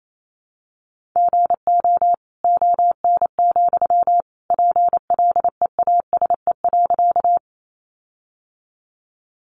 Morse code, heard as a keyed tone, about 28 wpm, 700 Hz. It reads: GO ON, PLEASE.
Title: Sentences from Top 1500 Words 28wpm